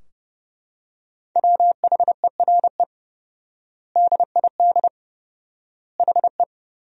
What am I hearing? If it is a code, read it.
Morse code, heard as a keyed tone, about 30 wpm, 700 Hz. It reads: WHERE DID HE
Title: 4x Word-Spacing - Sentences from Top 200 Words - 30wpm